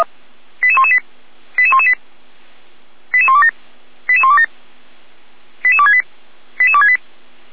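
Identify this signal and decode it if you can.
Five-tone selective call (ZVEI), heard as quick runs of five tones, tones of about 70 ms. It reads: 88118 88118 88127 88127 88268 88268
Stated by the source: ZVEI five-tone selective call received by SDR, Feuerwehr Orlamünde.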